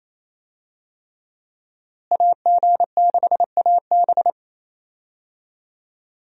Morse code, heard as a keyed tone, about 27 wpm, 700 Hz. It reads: AG6AB
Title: USA Call Signs - 5 Characters - Repeated 3x - 28wpm